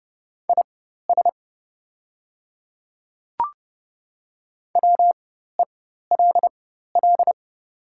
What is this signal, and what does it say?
Morse code, heard as a keyed tone, about 30 wpm, spaced as for 14 wpm, 700 Hz. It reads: IS WELL
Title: 4x Character Spacing - Top 200 Words - 30wpm